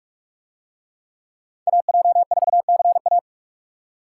Morse code, heard as a keyed tone, about 43 wpm, 700 Hz. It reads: AJ4CA